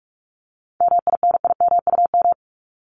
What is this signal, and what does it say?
Morse code, heard as a keyed tone, about 44 wpm, 700 Hz. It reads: MINIMUM
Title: Top 1251-1500 Words 45wpm